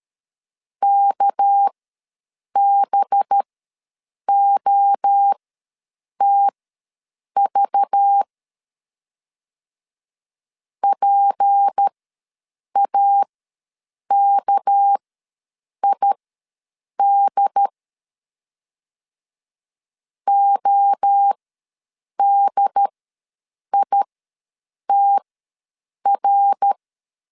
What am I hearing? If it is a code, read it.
Morse code, heard as a keyed tone, about 13 wpm, spaced as for 7 wpm, 800 Hz. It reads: KBOTV PAKID ODITR
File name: LEZIONI / 6 PKX0 / 6 PK ws 12wpm 3116